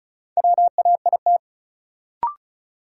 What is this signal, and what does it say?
Morse code, heard as a keyed tone, about 36 wpm, 700 Hz. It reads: WAIT
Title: Sentences from Top 500 Words - New Word Drill - 35wpm